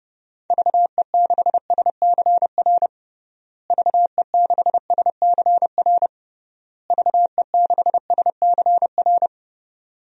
Morse code, heard as a keyed tone, about 30 wpm, 700 Hz. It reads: VE6SCR VE6SCR VE6SCR
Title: DXCC-100 Call Signs - 6 Characters - Repeated 3x - 30wpm